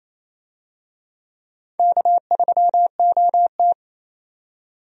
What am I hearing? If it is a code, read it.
Morse code, heard as a keyed tone, about 28 wpm, 700 Hz. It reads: K3OT
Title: USA Call Signs - 4 Characters - 28wpm